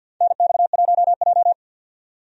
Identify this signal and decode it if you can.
Morse code, heard as a keyed tone, about 47 wpm, 700 Hz. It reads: NX1J
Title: USA Call Signs - 4 Characters - 50wpm